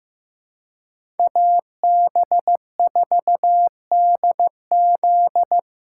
Morse code, heard as a keyed tone, about 16 wpm, 700 Hz. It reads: AB4DZ
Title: USA Call Signs - 4x Character Spacing - 15wpm